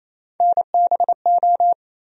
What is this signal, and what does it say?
Morse code, heard as a keyed tone, about 28 wpm, 700 Hz. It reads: NBO
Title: USA Call Sign Postfixes - 3 Letters 28wpm